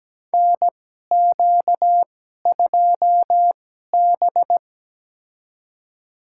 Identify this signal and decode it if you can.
Morse code, heard as a keyed tone, about 17 wpm, spaced as for 12 wpm, 700 Hz. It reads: NQ2B
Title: USA Call Signs - 4 Characters - 2x Character Spacing - 17wpm